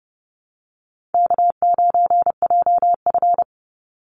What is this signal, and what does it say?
Morse code, heard as a keyed tone, about 30 wpm, 700 Hz. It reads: K9JF
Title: Easy Call Signs Repeated 3x - 30wpm